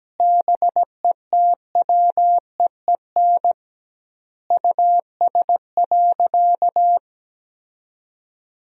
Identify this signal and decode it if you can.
Morse code, heard as a keyed tone, about 17 wpm, 700 Hz. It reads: BETWEEN US.
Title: 2x Word-Spacing - Sentences from Top 500 Words - 17wpm